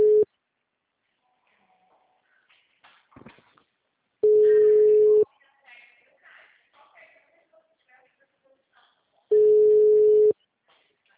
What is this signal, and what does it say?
Ringback tone (Europe, 425 Hz), with speech.
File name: out- -2013- - - .6435